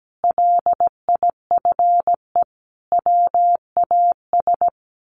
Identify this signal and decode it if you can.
Morse code, heard as a keyed tone, about 17 wpm, 700 Hz. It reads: LIFE WAS
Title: Sentences from Top 1400 Words 17wpm